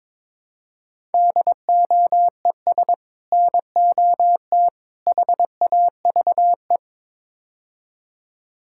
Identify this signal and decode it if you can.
Morse code, heard as a keyed tone, about 22 wpm, 700 Hz. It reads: DOES NOT HAVE
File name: practice / 22wpm / Build A Phrase - 5-Word Phrases - 22wpm